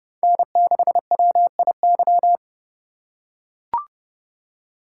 Morse code, heard as a keyed tone, about 30 wpm, 700 Hz. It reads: N6WIY
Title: USA Call Signs - 5 Characters - 3x Character Spacing - 30wpm